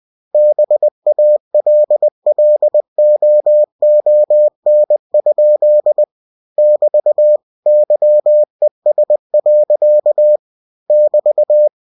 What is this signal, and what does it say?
Morse code, heard as a keyed tone, about 20 wpm, 600 Hz. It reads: BALLOON? <BT> YES. <BT>
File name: cw-books / winnie-the-pooh / winnie-20wpm-page-0056